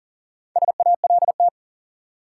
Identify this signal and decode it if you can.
Morse code, heard as a keyed tone, about 40 wpm, 700 Hz. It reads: SALT
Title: Four Letter Words - 40wpm